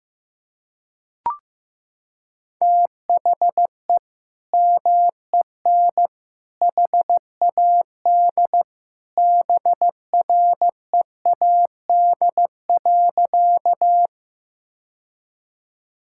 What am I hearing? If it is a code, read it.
Morse code, heard as a keyed tone, about 15 wpm, 700 Hz. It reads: THE MEN HAD BREAD.
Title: Sentences from Top 1600 Words 15wpm